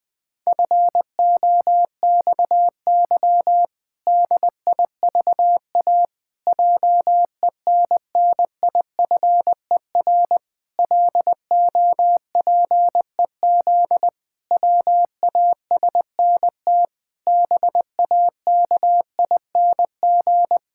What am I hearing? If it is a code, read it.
Morse code, heard as a keyed tone, about 20 wpm, 700 Hz. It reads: FOXY DIVA JENNIFER LOPEZ WASNT BAKING
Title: Pangram Challenge v2 - 20wpm